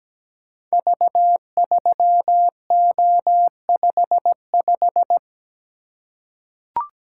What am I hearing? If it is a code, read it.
Morse code, heard as a keyed tone, about 17 wpm, 700 Hz. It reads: V3O55
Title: Code Groups - 5 Letter:Numbers - 17wpm